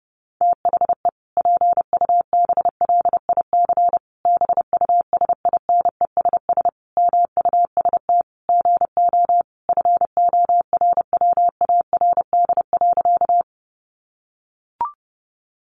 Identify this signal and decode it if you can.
Morse code, heard as a keyed tone, about 30 wpm, 700 Hz. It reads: THE PUBLIC BUSINESS MUST GO FORWARD.
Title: Sentences from Top 1300 Words 30wpm